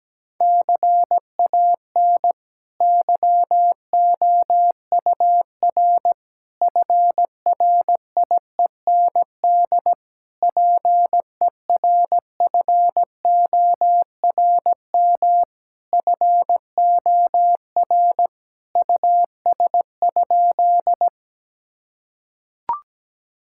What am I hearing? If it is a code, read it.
Morse code, heard as a keyed tone, about 17 wpm, 700 Hz. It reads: CAN YOUR FRIEND PERFORM FOR US?
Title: Sentences from Top 1000 Words - Review Part 5 - 17wpm